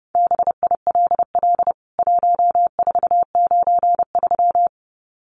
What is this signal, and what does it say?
Morse code, heard as a keyed tone, about 30 wpm, 700 Hz. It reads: BILL 1493
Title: CWT Exchange 30wpm